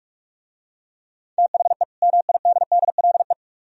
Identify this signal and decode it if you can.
Morse code, heard as a keyed tone, about 44 wpm, 700 Hz. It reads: THE MIDDLE